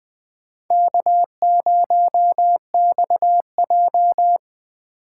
Morse code, heard as a keyed tone, about 21 wpm, 700 Hz. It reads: K0XJ